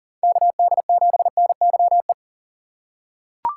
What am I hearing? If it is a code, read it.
Morse code, heard as a keyed tone, about 37 wpm, 700 Hz. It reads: KD7NYE
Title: USA Call Signs - 6 Characters - 6x Character Spacing - 40wpm